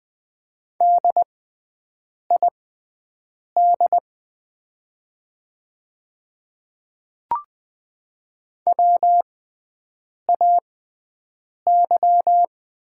Morse code, heard as a keyed tone, about 20 wpm, spaced as for 7 wpm, 700 Hz. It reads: DID WAY
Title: 6x Character Spacing - Top 200 Words - 20wpm